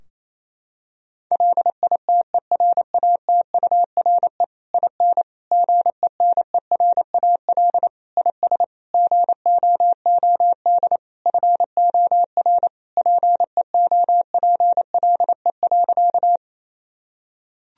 Morse code, heard as a keyed tone, about 28 wpm, 700 Hz. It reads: LITERATURE IN GENERAL IS GOOD FOR PEOPLE.
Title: Sentences from Top 1000 Words - Review Part 3 - 28wpm